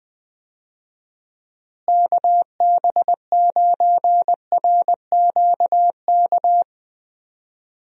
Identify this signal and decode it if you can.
Morse code, heard as a keyed tone, about 21 wpm, 700 Hz. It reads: KB9RQK
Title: USA Call Signs - 6 Characters - Repeated 2x - 20wpm